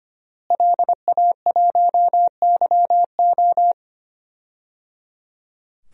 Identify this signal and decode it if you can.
Morse code, heard as a keyed tone, about 25 wpm, 700 Hz. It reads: LA1YO
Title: DXCC-100 Call Signs - 5 Characters - Repeated 3x - 25wpm